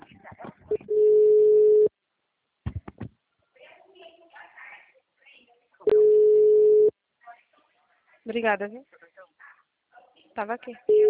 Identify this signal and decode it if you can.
Ringback tone (Europe, 425 Hz), with speech.
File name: out- -2010- - - .87917